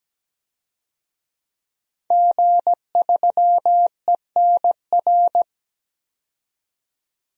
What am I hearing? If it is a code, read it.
Morse code, heard as a keyed tone, about 17 wpm, 700 Hz. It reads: G3ENR
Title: DXCC-100 Call Signs - 5 Characters - 17wpm